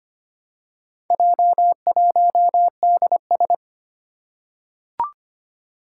Morse code, heard as a keyed tone, about 25 wpm, 700 Hz. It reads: J1DS
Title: Four Challenging Letter-Number Combinations 25wpm